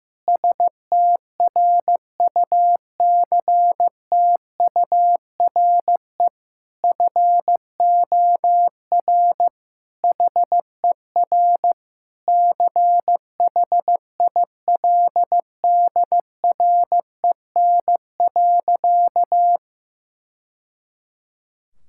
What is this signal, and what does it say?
Morse code, heard as a keyed tone, about 15 wpm, 700 Hz. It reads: STRUCTURE FOR HER CHILDREN.
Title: Sentences from Top 1000 Words - Review Part 6 - 15wpm